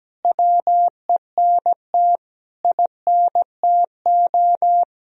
Morse code, heard as a keyed tone, about 17 wpm, 700 Hz. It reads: WENT INTO
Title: Sentences from Top 1000 Words - Review Part 2 - 17wpm